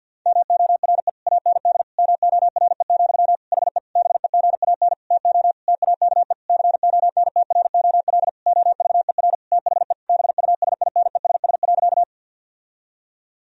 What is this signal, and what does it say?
Morse code, heard as a keyed tone, about 50 wpm, 700 Hz. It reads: MORE AND MORE, HE BEGAN TO TAKE CONTROL OVER THE BUSINESS.
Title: Sentences from Top 1000 Words - Review Part 3 - 50wpm